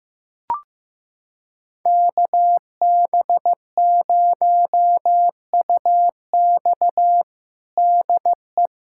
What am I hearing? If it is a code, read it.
Morse code, heard as a keyed tone, about 16 wpm, 700 Hz. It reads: KB0UX DE
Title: USA Callsign DE Callsign 15wpm